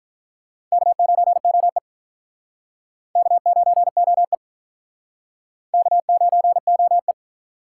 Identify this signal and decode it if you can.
Morse code, heard as a keyed tone, about 46 wpm, 700 Hz. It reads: K9OE K9OE K9OE